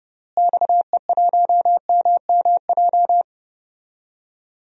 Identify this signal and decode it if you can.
Morse code, heard as a keyed tone, about 31 wpm, 700 Hz. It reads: XE1MMJ
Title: DXCC-100 Call Signs - 6 Characters - Repeated 2x - 30wpm